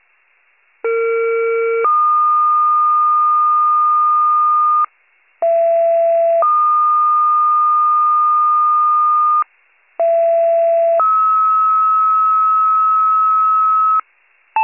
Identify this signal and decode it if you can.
Two-tone page, heard as a one-second tone, then a three-second tone, about 460 then 1220 Hz, about 670 then 1220 Hz, about 670 then 1290 Hz. It is heard from 0.8 to 14.0 s.